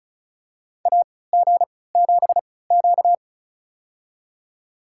Morse code, heard as a keyed tone, about 36 wpm, spaced as for 20 wpm, 690 Hz. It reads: AG7Q
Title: USA Call Signs - 3x Character Spacing - 35wpm